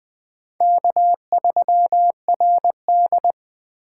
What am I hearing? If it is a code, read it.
Morse code, heard as a keyed tone, about 21 wpm, 700 Hz. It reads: K3RD
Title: USA Call Signs - 4 Characters - 4x Character Spacing - 20wpm